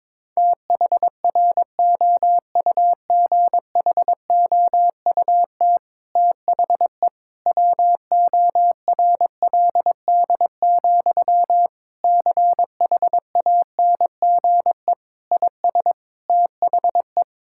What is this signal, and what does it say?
Morse code, heard as a keyed tone, about 22 wpm, 700 Hz. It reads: THROUGHOUT THE WORLD, CHANGE IS THE
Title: Sentences from Top 1000 Words - Review Part 5 - 22wpm